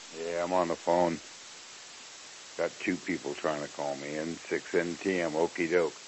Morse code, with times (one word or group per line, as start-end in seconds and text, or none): none